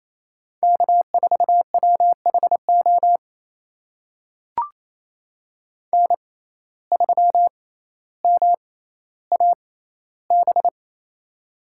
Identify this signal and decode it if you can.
Morse code, heard as a keyed tone, about 28 wpm, spaced as for 14 wpm, 700 Hz. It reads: K4WHO N3MAB
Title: USA Call Signs - 5 Characters - 6x Character Spacing - 28wpm